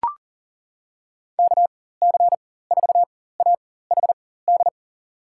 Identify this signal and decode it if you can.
Morse code, heard as a keyed tone, about 37 wpm, spaced as for 18 wpm, 700 Hz. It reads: KC4AHD
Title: USA Call Signs - 6 Characters - 4x Character Spacing - 40wpm